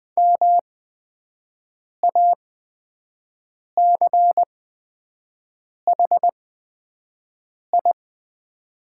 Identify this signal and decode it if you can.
Morse code, heard as a keyed tone, about 20 wpm, spaced as for 5 wpm, 700 Hz. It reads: MACHI
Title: 8x Character Spacing - Top 500 Words - 20wpm